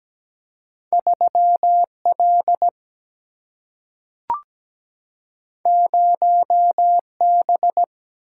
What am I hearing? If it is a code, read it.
Morse code, heard as a keyed tone, about 17 wpm, 700 Hz. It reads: 3L 0B
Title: Single-Letter Single-Number 17wpm